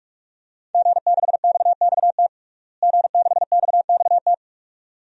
Morse code, heard as a keyed tone, about 46 wpm, 700 Hz. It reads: G6XXT G6XXT
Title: DXCC-100 Call Signs - 5 Characters - Repeated 2x - 45wpm